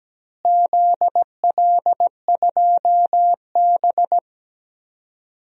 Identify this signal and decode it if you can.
Morse code, heard as a keyed tone, about 17 wpm, 700 Hz. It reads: ZL2B
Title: DXCC-100 Call Signs - 4 Characters - Repeated 2x - 17wpm